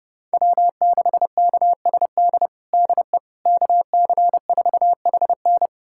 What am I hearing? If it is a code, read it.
Morse code, heard as a keyed tone, about 32 wpm, 700 Hz. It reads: W6KSD DE KC4HN
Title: USA Callsign DE Callsign - 5 Characters - 30wpm